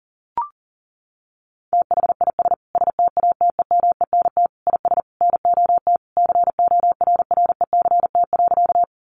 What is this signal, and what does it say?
Morse code, heard as a keyed tone, about 37 wpm, 700 Hz. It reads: THIS STATEMENT IS NOT CORRECT.
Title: Sentences from Top 1200 Words 40wpm